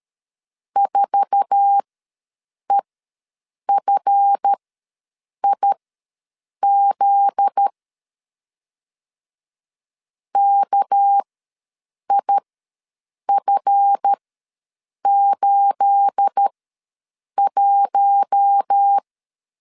Morse code, heard as a keyed tone, about 13 wpm, spaced as for 7 wpm, 800 Hz. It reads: 4EFIZ KIF81
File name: LEZIONI / 9 2867 ws 12wpm 3431